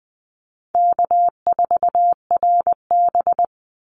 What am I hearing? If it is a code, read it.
Morse code, heard as a keyed tone, about 20 wpm, 700 Hz. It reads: K4RB
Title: Easy Call Signs Repeated 3x - 20wpm